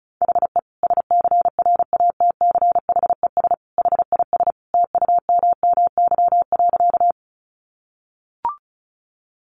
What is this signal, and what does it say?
Morse code, heard as a keyed tone, about 34 wpm, 700 Hz. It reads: HE SCRATCHES HIS TUMMY.
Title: Easy Phrases 35wpm